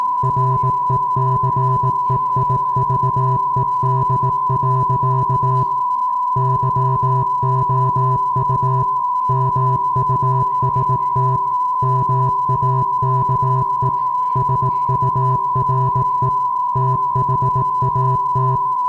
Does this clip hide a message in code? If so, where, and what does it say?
Yes: Morse code from 0.2 to 18.6 s, heard as a keyed tone, about 18 wpm, 390 Hz. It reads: RECEIVED. YOU MUST MAKE SURE THAT